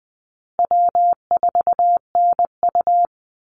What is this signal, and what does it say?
Morse code, heard as a keyed tone, about 20 wpm, 700 Hz. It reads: W4NU
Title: Easy Call Signs Repeated 3x - 20wpm